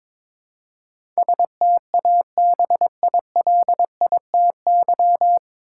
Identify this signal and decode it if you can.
Morse code, heard as a keyed tone, about 21 wpm, 700 Hz. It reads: STABILITY
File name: 9 Letter Words - First 3 Letters Given - 22wpm